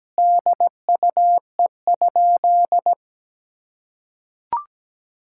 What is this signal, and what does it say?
Morse code, heard as a keyed tone, about 17 wpm, 700 Hz. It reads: DUE?
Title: Sentences from Top 1000 Words - Review Part 6 - 17wpm